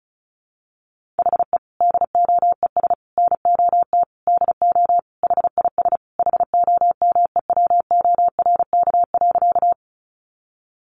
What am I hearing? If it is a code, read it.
Morse code, heard as a keyed tone, about 34 wpm, 700 Hz. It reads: HE DOES NOT DO HIS HOMEWORK.
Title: Additional Easy Phrases 35wpm